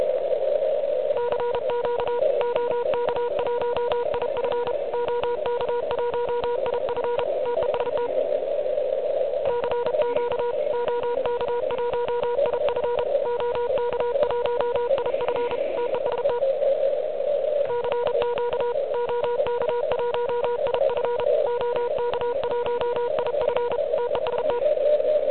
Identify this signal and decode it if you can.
Morse code, heard as a keyed tone, about 33 wpm, 970 Hz, with noise. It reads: CQ OK1IF OK1IF TEST CQ OK1IF OK1IF TEST CQ OK1IF OK1IF TEST
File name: ARCHIV / QCX / QCX Pileup